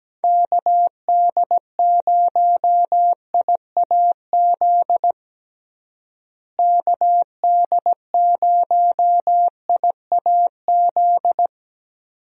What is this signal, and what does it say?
Morse code, heard as a keyed tone, about 17 wpm, 700 Hz. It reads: KD0IAZ KD0IAZ
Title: USA Call Signs - 6 Characters - Repeated 2x - 17wpm